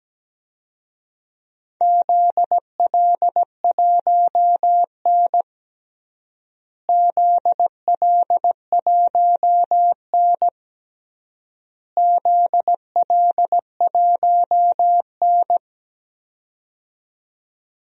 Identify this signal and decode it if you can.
Morse code, heard as a keyed tone, about 17 wpm, 700 Hz. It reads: ZL1N ZL1N ZL1N
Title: DXCC-100 Call Signs - 4 Characters - Repeated 3x - 17wpm